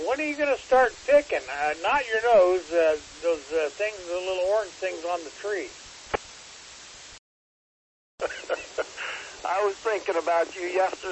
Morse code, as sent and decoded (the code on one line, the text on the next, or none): none